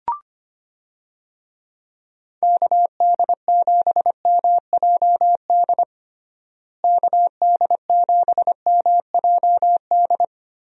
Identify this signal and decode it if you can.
Morse code, heard as a keyed tone, about 25 wpm, 700 Hz. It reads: KD7MJD KD7MJD
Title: USA Call Signs - 6 Characters - Repeated 3x - 25wpm